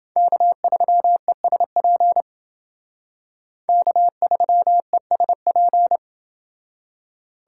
Morse code, heard as a keyed tone, about 29 wpm, 700 Hz. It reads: K3ESP K3ESP